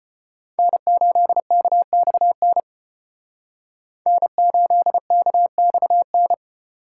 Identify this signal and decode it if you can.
Morse code, heard as a keyed tone, about 31 wpm, 700 Hz. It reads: N8KXN N8KXN
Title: Speed Racing - USA 1x3 Call Signs 25wpm